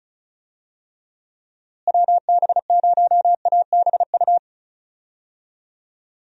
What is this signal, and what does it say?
Morse code, heard as a keyed tone, about 34 wpm, 700 Hz. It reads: WB0ABU
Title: USA Call Signs - 6 Characters - Repeated 3x - 35wpm